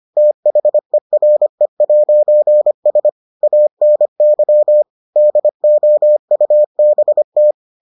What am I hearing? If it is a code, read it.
Morse code, heard as a keyed tone, about 25 wpm, 600 Hz. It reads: THERE'S ANY DOUBT